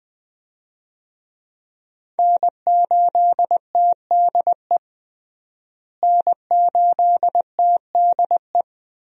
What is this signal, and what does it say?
Morse code, heard as a keyed tone, about 20 wpm, 700 Hz. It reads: N8TDE N8TDE
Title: USA Call Signs - 5 Characters - Repeated 3x - 20wpm